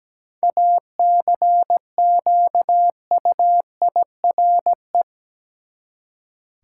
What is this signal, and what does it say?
Morse code, heard as a keyed tone, about 17 wpm, 700 Hz. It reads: ACQUIRE
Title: Seven Letter Words - 17wpm